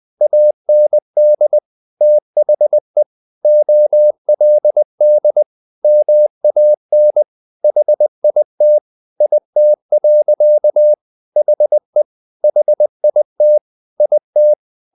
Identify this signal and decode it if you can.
Morse code, heard as a keyed tone, about 20 wpm, 600 Hz. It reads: AND THE OLD MAN HIT IT. HE HIT IT
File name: the old man and the sea part 2 20wpm